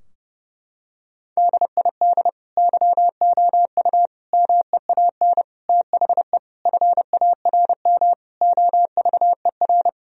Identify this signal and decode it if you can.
Morse code, heard as a keyed tone, about 30 wpm, 700 Hz. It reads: DID YOU MEAN THE FARM OVER
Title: Sentences from Top 1000 Words - Review Part 2 - 30wpm